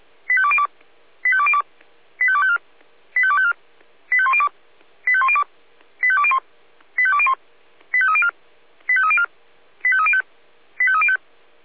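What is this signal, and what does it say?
Five-tone selective call (ZVEI), heard as quick runs of five tones, tones of about 70 ms. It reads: 85202 85202 85254 85254 85102 85102 85201 85201 85304 85304 85305 85305
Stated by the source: Five-tone ZVEI selective call received on SDR.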